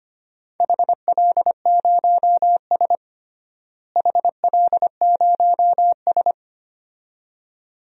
Morse code, heard as a keyed tone, about 25 wpm, 700 Hz. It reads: HL0S HL0S
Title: DXCC-100 Call Signs - 4 Characters - Repeated 2x - 25wpm